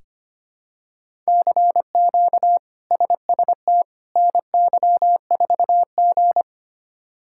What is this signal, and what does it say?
Morse code, heard as a keyed tone, about 25 wpm, 700 Hz. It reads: CQ SST NY4G
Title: K1USN SST QSO Practice - 25wpm